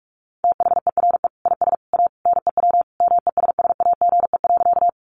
Morse code, heard as a keyed tone, about 44 wpm, 700 Hz. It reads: THERE IS A NEW MESSAGE.
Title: Sentences from Top 1100 Words 45wpm